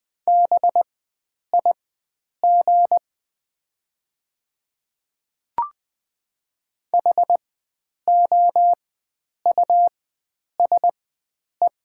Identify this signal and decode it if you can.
Morse code, heard as a keyed tone, about 20 wpm, spaced as for 9 wpm, 700 Hz. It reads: BIG HOUSE
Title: 4x Character Spacing - Top 300 Words - 20wpm